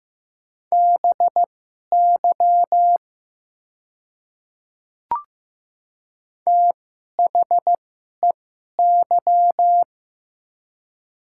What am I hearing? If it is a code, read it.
Morse code, heard as a keyed tone, about 15 wpm, spaced as for 11 wpm, 700 Hz. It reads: BY THEY
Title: 2x Character Spacing - Top 100 Words - 15wpm